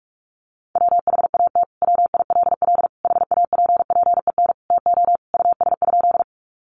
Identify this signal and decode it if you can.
Morse code, heard as a keyed tone, about 46 wpm, 700 Hz. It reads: WHAT WILL HAPPEN TO US?
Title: Sentences from Top 400 Words 45wpm